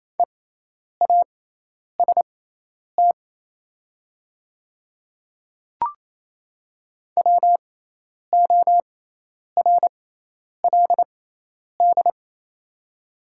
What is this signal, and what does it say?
Morse code, heard as a keyed tone, about 28 wpm, spaced as for 10 wpm, 700 Hz. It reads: EAST WORLD
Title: 6x Character Spacing - Top 500 Words - 28wpm